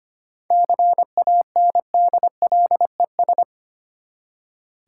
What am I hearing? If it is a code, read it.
Morse code, heard as a keyed tone, about 25 wpm, 690 Hz. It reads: CANDLES
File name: practice / 25wpm / Easy Words with Hints - 25wpm